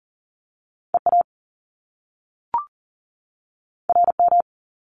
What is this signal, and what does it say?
Morse code, heard as a keyed tone, about 42 wpm, 700 Hz. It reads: EA RM
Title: Top Two Letter Combinations in Words 40wpm